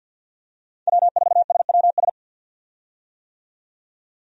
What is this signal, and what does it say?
Morse code, heard as a keyed tone, about 50 wpm, 700 Hz. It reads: W4SWS
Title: USA Call Signs - 5 Characters - 50wpm